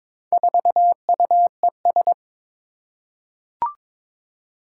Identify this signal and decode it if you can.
Morse code, heard as a keyed tone, about 22 wpm, 700 Hz. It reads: 4UES
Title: Four Challenging Letter-Number Combinations 22wpm